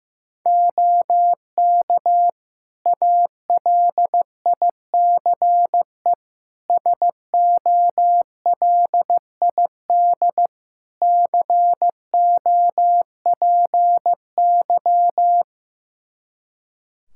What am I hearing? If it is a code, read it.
Morse code, heard as a keyed tone, about 15 wpm, 700 Hz. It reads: OK ALICE SOLID COPY
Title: Practice QSO - Exchange 3 - More Info - 15wpm